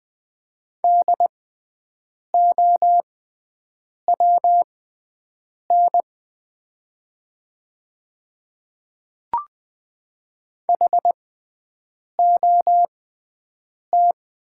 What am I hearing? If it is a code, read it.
Morse code, heard as a keyed tone, about 20 wpm, spaced as for 7 wpm, 700 Hz. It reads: DOWN HOT
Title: 6x Character Spacing - Top 100 Words - 20wpm